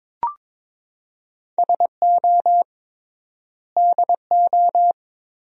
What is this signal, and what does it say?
Morse code, heard as a keyed tone, about 22 wpm, 700 Hz. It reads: SO DO
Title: 3x Word-Spacing - Sentences from Top 400 Words - 22wpm